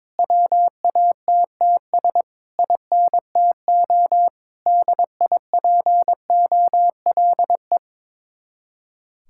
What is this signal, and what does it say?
Morse code, heard as a keyed tone, about 22 wpm, 700 Hz. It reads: WATTS INTO DIPOLE
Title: Speed-Racing Common QSO Phrases - 22wpm